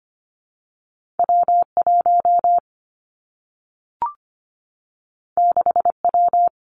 Morse code, heard as a keyed tone, about 25 wpm, 700 Hz. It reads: W 1 6 W
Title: Two Challenging Letter-Number Combinations 25wpm